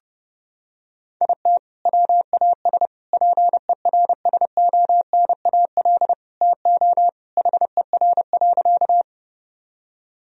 Morse code, heard as a keyed tone, about 30 wpm, 700 Hz. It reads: IT WAS PERSONAL TO HER.
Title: Sentences from Top 1000 Words - Review Part 4 - 30wpm